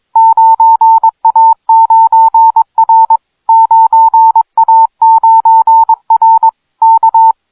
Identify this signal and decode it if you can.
Morse code, heard as a keyed tone, about 22 wpm, 900 Hz. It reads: 9A9R 9A9R K